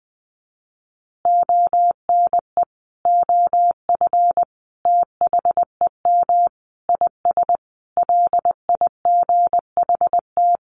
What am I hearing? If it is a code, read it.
Morse code, heard as a keyed tone, about 20 wpm, 700 Hz. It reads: ONE OF THEM IS LIGHT
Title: Additional Easy Phrases 20wpm